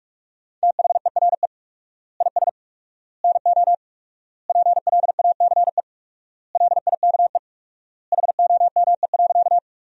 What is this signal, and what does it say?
Morse code, heard as a keyed tone, about 43 wpm, 700 Hz. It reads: THERE IS NO PLACE LIKE HOME.
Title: 4x Word-Spacing - Sentences from Top 200 Words - 45wpm